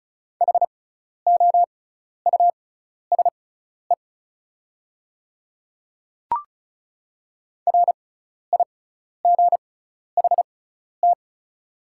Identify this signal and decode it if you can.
Morse code, heard as a keyed tone, about 36 wpm, spaced as for 12 wpm, 690 Hz. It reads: HOUSE RIGHT
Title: 6x Character Spacing - Top 300 Words - 35wpm